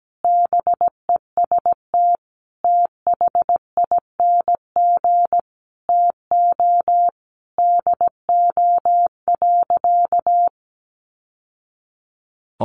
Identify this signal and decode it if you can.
Morse code, heard as a keyed tone, about 17 wpm, 700 Hz. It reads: BEST THING TO DO.
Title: Sentences from Top 1200 Words 17wpm